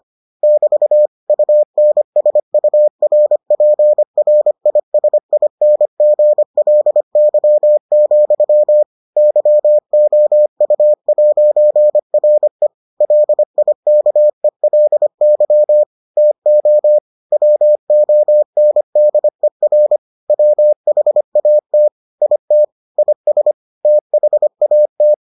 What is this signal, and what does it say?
Morse code, heard as a keyed tone, about 25 wpm, 600 Hz. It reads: <BT> UNSURPRISINGLY, YOU'RE LIKELY TO WONDER WHAT IT IS THAT